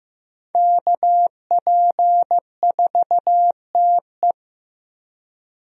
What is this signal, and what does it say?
Morse code, heard as a keyed tone, about 15 wpm, 700 Hz. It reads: KP4TE
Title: DXCC-100 Call Signs - 5 Characters - Repeated 2x - 15wpm